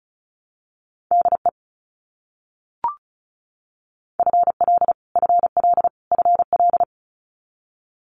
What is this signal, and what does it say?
Morse code, heard as a keyed tone, about 34 wpm, 700 Hz. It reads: DE FL FL FL
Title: US States Repeated 3x 35wpm